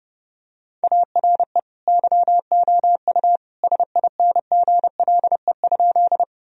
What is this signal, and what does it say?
Morse code, heard as a keyed tone, about 30 wpm, 700 Hz. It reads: ARE YOU SINGLE?
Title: Sentences from Top 1000 Words - Review Part 4 - 30wpm